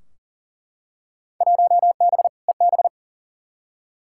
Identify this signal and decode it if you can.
Morse code, heard as a keyed tone, about 39 wpm, 700 Hz. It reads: 1B EB
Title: Field Day Exchanges Encore - 40wpm